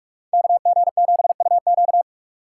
Morse code, heard as a keyed tone, about 43 wpm, 700 Hz. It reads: KG7UQ